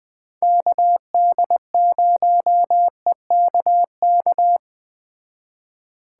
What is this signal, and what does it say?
Morse code, heard as a keyed tone, about 20 wpm, 700 Hz. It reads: KD0EKK
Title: Speed Racing - USA Call Signs 17wpm